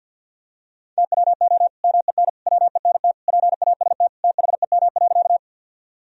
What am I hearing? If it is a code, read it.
Morse code, heard as a keyed tone, about 51 wpm, 700 Hz. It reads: TWO MEN WENT PAST THEM.
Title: Sentences from Top 1000 Words - Review Part 3 - 50wpm